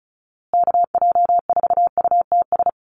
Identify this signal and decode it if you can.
Morse code, heard as a keyed tone, about 34 wpm, 700 Hz. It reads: KJ4UTS
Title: USA Call Signs Repeated 3x 35wpm